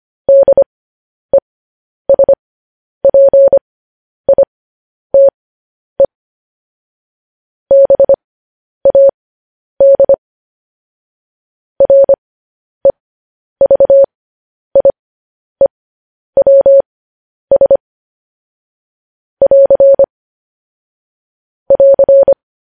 Morse code, heard as a keyed tone, about 25 wpm, spaced as for 10 wpm, 550 Hz. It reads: DESPITE BAD REVIEWS <AR> <AR>
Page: news-headlines-in-morse-code-at-10-wpm